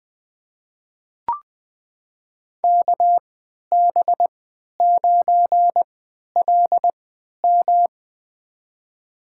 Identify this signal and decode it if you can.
Morse code, heard as a keyed tone, about 20 wpm, spaced as for 11 wpm, 700 Hz. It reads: KB9LM
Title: USA Call Signs - 5 Characters - 3x Character Spacing - 20wpm